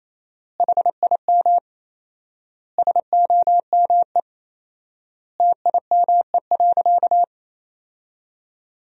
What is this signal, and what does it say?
Morse code, heard as a keyed tone, about 28 wpm, 700 Hz. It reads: HIM SOME TIME.
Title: 4x Word-Spacing - Sentences from Top 300 Words - 28wpm